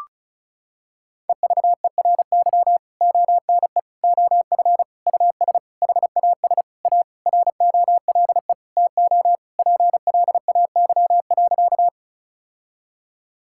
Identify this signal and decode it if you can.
Morse code, heard as a keyed tone, about 36 wpm, 700 Hz. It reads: EVERY ONE OF US HAS A ROLE TO PLAY.
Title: Sentences from Top 1000 Words - Review Part 6 - 35wpm